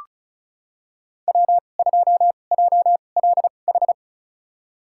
Morse code, heard as a keyed tone, about 36 wpm, spaced as for 25 wpm, 700 Hz. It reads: W2JLH
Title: USA Call Signs - 5 Characters - 2x Character Spacing - 35wpm